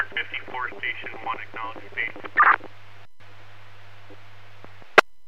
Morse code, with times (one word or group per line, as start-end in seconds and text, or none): none